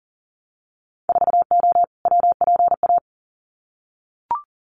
Morse code, heard as a keyed tone, about 40 wpm, 700 Hz. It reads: 4O WPA
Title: Winter Field Day Exchanges 40wpm